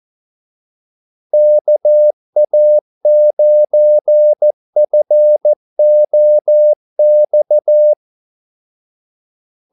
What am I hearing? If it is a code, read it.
Morse code, heard as a keyed tone, about 14 wpm, 600 Hz. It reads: KA9FOX
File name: Callsigns-14wpm